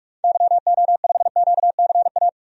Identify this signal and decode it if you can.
Morse code, heard as a keyed tone, about 44 wpm, 700 Hz. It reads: YO5QCA
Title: DXCC-100 Call Signs - 6 Characters - 45wpm